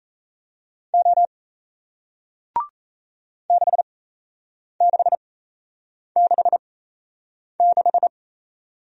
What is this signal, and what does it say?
Morse code, heard as a keyed tone, about 38 wpm, spaced as for 8 wpm, 700 Hz. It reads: O 6666